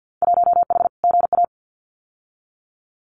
Morse code, heard as a keyed tone, about 50 wpm, 700 Hz. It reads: 1H GA
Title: Winter Field Day Exchanges 50wpm